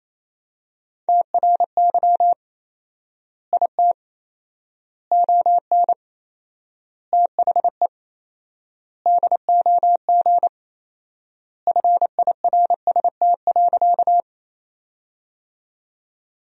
Morse code, heard as a keyed tone, about 28 wpm, 700 Hz. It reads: TRY IT ON THE DOG FIRST.
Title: 4x Word-Spacing - Sentences from Top 400 Words - 28wpm